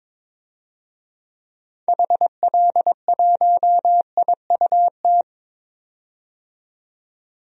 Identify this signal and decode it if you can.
Morse code, heard as a keyed tone, about 22 wpm, 700 Hz. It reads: HL1IUT
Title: DXCC-100 Call Signs - 6 Characters - 22wpm